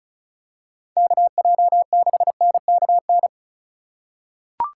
Morse code, heard as a keyed tone, about 36 wpm, 700 Hz. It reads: KJ6NKN
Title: USA Call Signs - 6 Characters - 3x Character Spacing - 35wpm